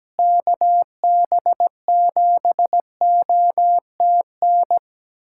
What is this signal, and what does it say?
Morse code, heard as a keyed tone, about 17 wpm, 700 Hz. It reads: KB7OTN
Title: USA Callsign DE Callsign 17wpm